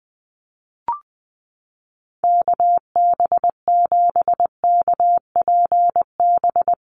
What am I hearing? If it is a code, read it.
Morse code, heard as a keyed tone, about 20 wpm, 700 Hz. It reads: KB7KPB
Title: USA Call Signs Repeated 3x 20wpm